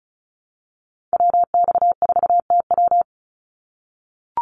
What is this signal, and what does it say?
Morse code, heard as a keyed tone, about 34 wpm, 700 Hz. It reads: WX4TW
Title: NA SOTA Activator Callsigns 35wpm